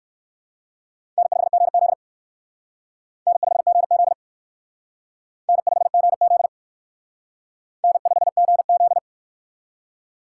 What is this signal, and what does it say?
Morse code, heard as a keyed tone, about 55 wpm, 700 Hz. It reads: N5GZ N5GZ N5GZ N5GZ
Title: Speed Racing - USA 1x2 Call Signs 45wpm